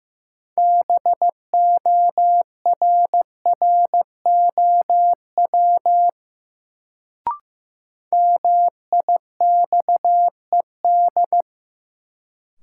Morse code, heard as a keyed tone, about 15 wpm, 700 Hz. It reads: BORROW MIXED
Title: Sentences from Top 1500 Words - New Word Drill - 15wpm